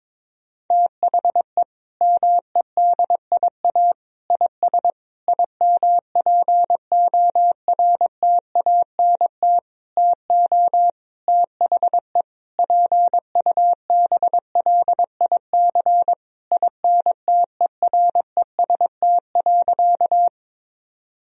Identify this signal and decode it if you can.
Morse code, heard as a keyed tone, about 22 wpm, 700 Hz. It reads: THE MEDIA IS IMPORTANT TO THE PUBLIC INTEREST.
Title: Sentences from Top 1000 Words - Review Part 4 - 22wpm